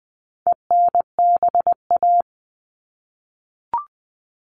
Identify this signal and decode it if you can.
Morse code, heard as a keyed tone, about 20 wpm, 700 Hz. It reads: ENBA
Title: All Cut Numbers - 4-Digit 20wpm